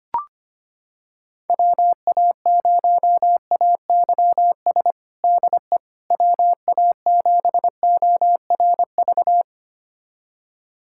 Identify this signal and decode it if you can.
Morse code, heard as a keyed tone, about 25 wpm, 700 Hz. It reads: WA0AYS DE WA7ORV
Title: USA Callsign DE Callsign - 6 Characters - 25wpm